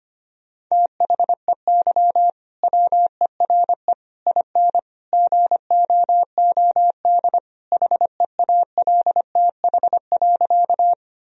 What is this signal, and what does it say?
Morse code, heard as a keyed tone, about 25 wpm, 700 Hz. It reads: THEY WERE IN GOOD HEALTH.
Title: Sentences from Top 1000 Words - Review Part 3 - 25wpm